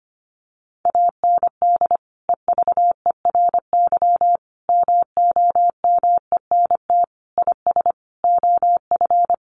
Morse code, heard as a keyed tone, about 25 wpm, 700 Hz. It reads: AND EVERY MOMENT IS OF
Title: Sentences from Top 1200 Words 25wpm